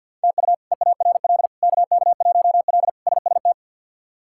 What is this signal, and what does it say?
Morse code, heard as a keyed tone, about 51 wpm, 700 Hz. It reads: TU EARL KK1L SST
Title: K1USN SST QSO Practice - 50wpm